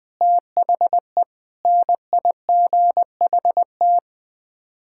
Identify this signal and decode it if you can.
Morse code, heard as a keyed tone, about 21 wpm, 700 Hz. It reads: THE NIGHT
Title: Sets of 8 Words - 20wpm